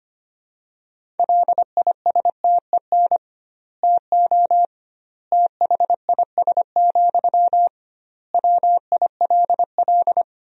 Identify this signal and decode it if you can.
Morse code, heard as a keyed tone, about 25 wpm, 700 Hz. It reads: LISTEN TO THIS, WILL
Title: 2x Word-Spacing - Sentences from Top 400 Words - 25wpm